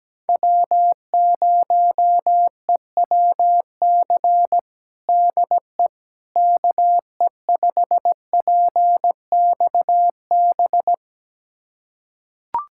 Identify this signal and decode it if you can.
Morse code, heard as a keyed tone, about 17 wpm, 700 Hz. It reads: W0EWC DE KE5PXB
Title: USA Callsign DE Callsign 17wpm